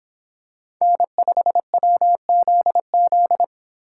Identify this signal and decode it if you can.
Morse code, heard as a keyed tone, about 26 wpm, 700 Hz. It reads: N5WZZ